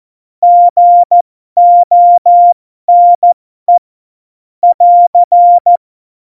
Morse code, heard as a keyed tone, about 14 wpm, spaced as for 12 wpm, 700 Hz. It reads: GONE <AR>
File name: Morsekurs / Witze